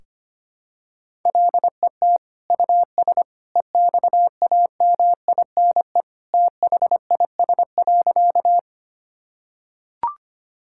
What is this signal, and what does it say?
Morse code, heard as a keyed tone, about 25 wpm, 700 Hz. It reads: LET US EXAMINE THIS.
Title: Sentences from Top 1600 Words 25wpm